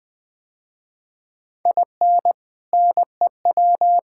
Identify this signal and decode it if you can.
Morse code, heard as a keyed tone, about 20 wpm, 700 Hz. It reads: IN NEW
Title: Sentences with 2-Word Phrases - Spoken Beginning - 20wpm